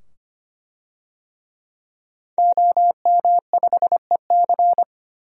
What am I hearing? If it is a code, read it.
Morse code, heard as a keyed tone, about 25 wpm, 700 Hz. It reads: OM5EC
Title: DXCC-100 Call Signs - 5 Characters - 25wpm